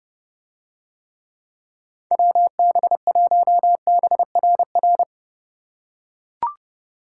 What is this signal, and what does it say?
Morse code, heard as a keyed tone, about 30 wpm, 700 Hz. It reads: WB1BRR